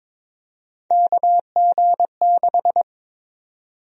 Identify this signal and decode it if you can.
Morse code, heard as a keyed tone, about 22 wpm, 700 Hz. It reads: KG6
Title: USA Call Sign Prefixes - 22wpm